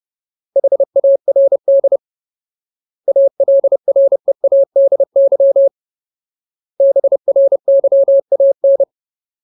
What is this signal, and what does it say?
Morse code, heard as a keyed tone, about 30 wpm, 550 Hz. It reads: HARD ALREADY BRYAN
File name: Ultimate Word List / 30WPM Ultimate